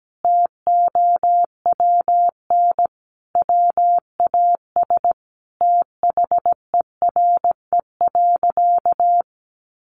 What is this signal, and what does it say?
Morse code, heard as a keyed tone, about 17 wpm, 700 Hz. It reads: TOWN WAS THERE.
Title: Sentences from Top 500 Words 17wpm